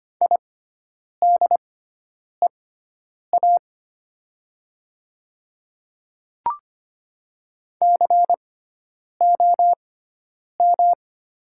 Morse code, heard as a keyed tone, about 25 wpm, spaced as for 9 wpm, 700 Hz. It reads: IDEA COM
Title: 6x Character Spacing - Top 500 Words - 25wpm